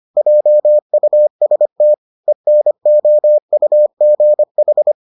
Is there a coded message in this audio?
Yes: Morse code from 0.2 to 4.9 s, heard as a keyed tone, about 25 wpm, 600 Hz. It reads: JUST ENOUGH